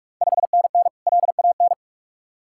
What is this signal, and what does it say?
Morse code, heard as a keyed tone, about 45 wpm, 700 Hz. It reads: 5NN LAN
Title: Ontario QSO Party Exchange 45wpm